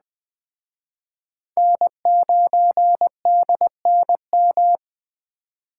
Morse code, heard as a keyed tone, about 20 wpm, 700 Hz. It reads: N9DNM
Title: Speed Racing - USA 1x3 Call Signs 17wpm